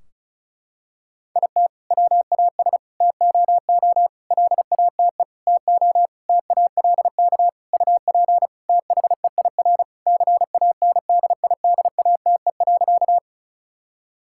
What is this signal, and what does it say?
Morse code, heard as a keyed tone, about 36 wpm, 700 Hz. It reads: IT WAS TOO LATE TO TALK UP THEIR CANDIDATE.
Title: Sentences from Top 1600 Words 35wpm